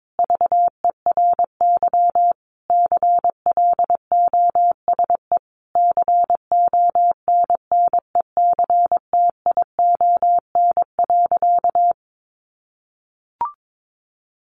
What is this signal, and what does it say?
Morse code, heard as a keyed tone, about 22 wpm, 700 Hz. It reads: VERY CLOSE CONNECTION.
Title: Sentences from Top 1300 Words 22wpm